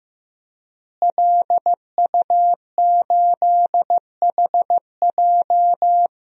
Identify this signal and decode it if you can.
Morse code, heard as a keyed tone, about 15 wpm, 700 Hz. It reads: LU8HJ
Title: DXCC-100 Call Signs - 5 Characters - Repeated 3x - 15wpm